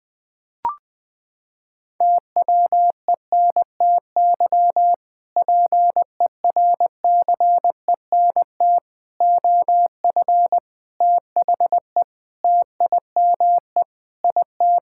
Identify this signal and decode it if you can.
Morse code, heard as a keyed tone, about 20 wpm, 700 Hz. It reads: TWENTY PERCENT OF THE TIME IT
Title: Sentences with 2-Word Phrases - 20wpm